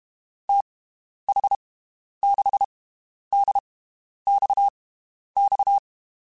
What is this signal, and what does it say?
Morse code, heard as a keyed tone, about 32 wpm, spaced as for 11 wpm, 800 Hz. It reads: TH6DXX
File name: (spaced) / QSOs Spaced-mp3s / WD0BGP 1 @30 sp